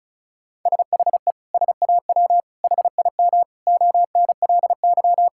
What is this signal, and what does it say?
Morse code, heard as a keyed tone, about 34 wpm, 700 Hz. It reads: SHE SAW HIM ONLY
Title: Sentences from Top 1600 Words 35wpm